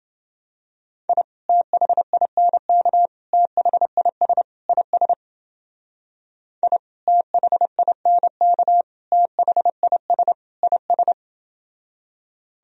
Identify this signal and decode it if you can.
Morse code, heard as a keyed tone, about 29 wpm, 700 Hz. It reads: I THINK THIS IS I THINK THIS IS